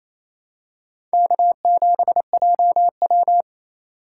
Morse code, heard as a keyed tone, about 27 wpm, 700 Hz. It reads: K7JW
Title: USA Call Signs - 4 Characters - 3x Character Spacing - 28wpm